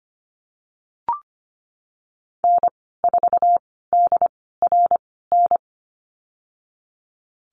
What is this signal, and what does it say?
Morse code, heard as a keyed tone, about 25 wpm, spaced as for 16 wpm, 700 Hz. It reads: N4DRN
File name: Lesson 15g - Number 5 - Call Signs 25wpm